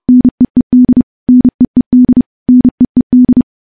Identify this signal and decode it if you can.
Morse code, heard as a keyed tone, about 30 wpm, 250 Hz. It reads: NEED NEED NEED
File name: morse / morse-training-30WPM